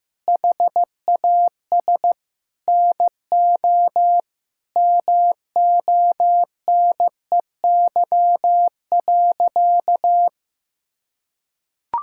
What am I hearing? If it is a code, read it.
Morse code, heard as a keyed tone, about 16 wpm, 700 Hz. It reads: HAS NO MONEY.
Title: Sentences from Top 1000 Words - Review Part 5 - 15wpm